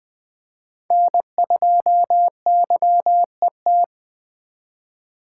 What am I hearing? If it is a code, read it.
Morse code, heard as a keyed tone, about 20 wpm, 700 Hz. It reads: N2YET
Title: Speed Racing - USA Call Signs 17wpm